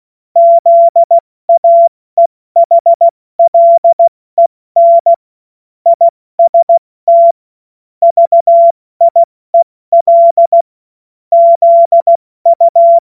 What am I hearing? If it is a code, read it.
Morse code, heard as a keyed tone, about 16 wpm, 680 Hz. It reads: ZAEHLEN IST VIEL ZU
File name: Morsekurs / Witze / Witz 07. Dezember 2021, 70 F80